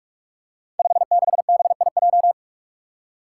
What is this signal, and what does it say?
Morse code, heard as a keyed tone, about 46 wpm, 700 Hz. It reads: 56BIJ
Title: Five Challenging Letter-Number Combinations 45wpm